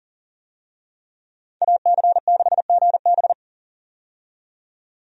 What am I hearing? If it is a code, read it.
Morse code, heard as a keyed tone, about 40 wpm, 700 Hz. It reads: AC6GB
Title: USA Call Signs - 5 Characters - 40wpm